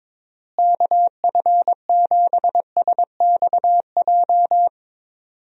Morse code, heard as a keyed tone, about 22 wpm, 700 Hz. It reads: KF7SXJ
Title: USA Call Signs - 6 Characters - Repeated 2x - 22wpm